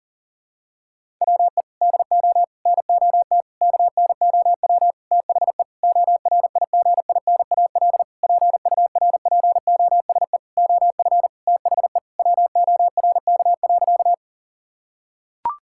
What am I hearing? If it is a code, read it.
Morse code, heard as a keyed tone, about 40 wpm, 700 Hz. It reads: WE DO NOT KNOW THE ORIGINAL PURPOSE OF THE WORK.
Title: Sentences from Top 1000 Words - Review Part 6 - 40wpm